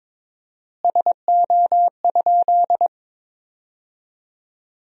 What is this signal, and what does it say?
Morse code, heard as a keyed tone, about 22 wpm, 700 Hz. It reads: SO?
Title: 4x Word-Spacing - Sentences from Top 200 Words - 22wpm